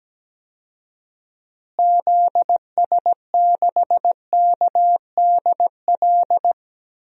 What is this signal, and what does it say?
Morse code, heard as a keyed tone, about 17 wpm, 700 Hz. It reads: ZS6KDL
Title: DXCC-100 Call Signs - 6 Characters - Repeated 2x - 17wpm